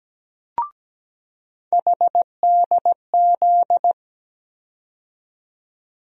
Morse code, heard as a keyed tone, about 17 wpm, 700 Hz. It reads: HDZ
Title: USA Call Sign Postfixes - 3 Letters 17wpm